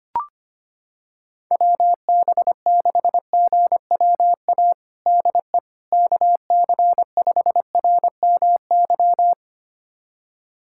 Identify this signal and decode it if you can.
Morse code, heard as a keyed tone, about 25 wpm, 700 Hz. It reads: WB6GWA DE KC5RMY
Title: USA Callsign DE Callsign 25wpm